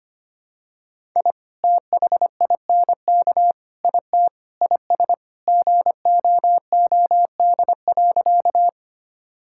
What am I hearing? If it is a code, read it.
Morse code, heard as a keyed tone, about 25 wpm, 700 Hz. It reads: I THINK IT IS GOOD.
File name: practice / 25wpm / Sentences with 3-Word Phrases - 25wpm